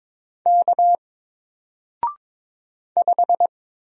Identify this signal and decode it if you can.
Morse code, heard as a keyed tone, about 22 wpm, 700 Hz. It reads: K 5
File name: Single-Letter-Number-22wpm